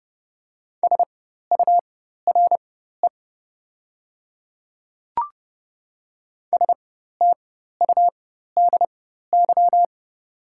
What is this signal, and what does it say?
Morse code, heard as a keyed tone, about 30 wpm, spaced as for 14 wpm, 700 Hz. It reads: SURE STUDY
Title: 4x Character Spacing - Top 300 Words - 30wpm